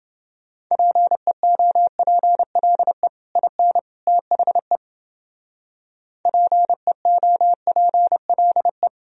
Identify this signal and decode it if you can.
Morse code, heard as a keyed tone, about 29 wpm, 700 Hz. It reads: PEOPLE IN THE PEOPLE